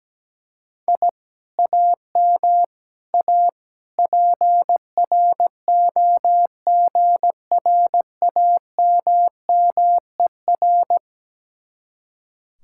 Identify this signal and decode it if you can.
Morse code, heard as a keyed tone, about 17 wpm, 700 Hz. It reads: I AM A PROGRAMMER
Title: Speed-Racing Common QSO Phrases - 17wpm